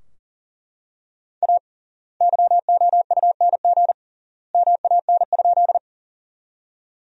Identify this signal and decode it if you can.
Morse code, heard as a keyed tone, about 40 wpm, 700 Hz. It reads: A YOUNG MAN?
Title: 3x Word-Spacing - Sentences from Top 400 Words - 40wpm